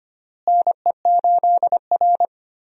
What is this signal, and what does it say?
Morse code, heard as a keyed tone, about 25 wpm, 700 Hz. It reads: NE8R
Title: USA Call Signs - 4 Characters - Repeated 3x - 25wpm